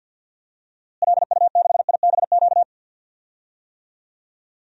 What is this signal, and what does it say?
Morse code, heard as a keyed tone, about 48 wpm, 700 Hz. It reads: LU6IBQ
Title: DXCC-100 Call Signs - 6 Characters - Repeated 2x - 50wpm